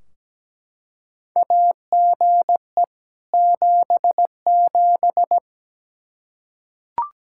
Morse code, heard as a keyed tone, about 17 wpm, 700 Hz. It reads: AGE 77
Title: Practice QSO - Exchange 3 - More Info - 17wpm